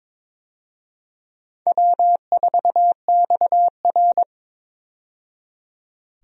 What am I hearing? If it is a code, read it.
Morse code, heard as a keyed tone, about 22 wpm, 700 Hz. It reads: W4XR
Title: USA Call Signs - 4 Characters - 22wpm